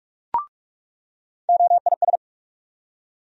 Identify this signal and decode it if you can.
Morse code, heard as a keyed tone, about 47 wpm, 700 Hz. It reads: OIS
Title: USA Call Sign Postfixes - 3 Letters 45wpm